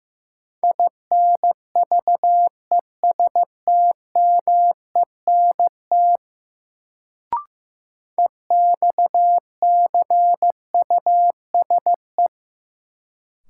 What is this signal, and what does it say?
Morse code, heard as a keyed tone, about 15 wpm, 700 Hz. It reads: INVESTMENT EXCUSE
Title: Sentences from Top 700 Words - New Word Drill - 15wpm